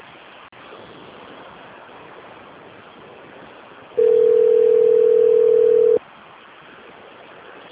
Ringback tone (North America), 440 + 480 Hz.
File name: rg-1600-unknown- - - .20831